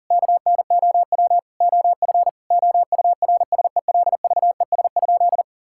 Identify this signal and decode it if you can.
Morse code, heard as a keyed tone, about 40 wpm, 700 Hz. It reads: KNOW OF OURSELVES?
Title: Sentences from Top 1600 Words 40wpm